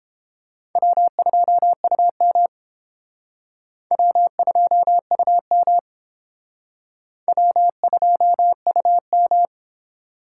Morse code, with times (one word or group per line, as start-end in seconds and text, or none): 0.8-2.5 W2UM
3.9-5.8 W2UM
7.3-9.5 W2UM